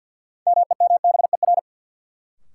Morse code, heard as a keyed tone, about 49 wpm, 700 Hz. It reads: MEMBER
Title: Sentences from Top 1100 Words - New Word Drill - 50wpm